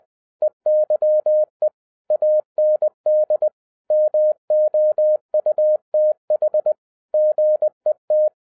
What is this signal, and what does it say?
Morse code, heard as a keyed tone, about 20 wpm, 610 Hz. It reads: EYE AND MOUTH GET
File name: 2011-07-mp3 / n58 BACK FROM A BIG ACCIDENT 20wpm